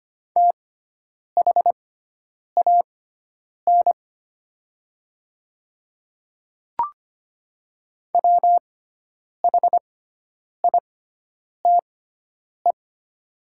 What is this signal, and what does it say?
Morse code, heard as a keyed tone, about 25 wpm, spaced as for 9 wpm, 700 Hz. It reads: THAN WHITE
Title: 6x Character Spacing - Top 300 Words - 25wpm